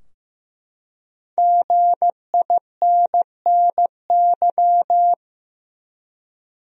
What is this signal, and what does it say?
Morse code, heard as a keyed tone, about 15 wpm, 700 Hz. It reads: GINNY
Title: USA Common Names - Max 5 Letters - 15wpm